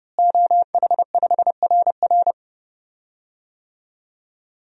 Morse code, heard as a keyed tone, about 30 wpm, 700 Hz. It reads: OH5RR
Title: DXCC-100 Call Signs - 5 Characters - Repeated 3x - 30wpm